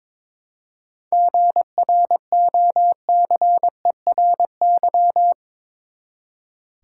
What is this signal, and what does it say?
Morse code, heard as a keyed tone, about 22 wpm, 700 Hz. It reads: GROCERY
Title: Seven Letter Words - 22wpm